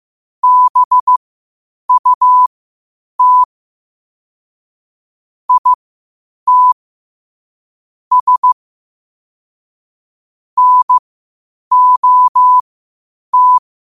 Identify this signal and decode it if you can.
Morse code, heard as a keyed tone, about 15 wpm, spaced as for 8 wpm, 1000 Hz. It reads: BUT IT S NOT
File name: qotdfiles / 8wpm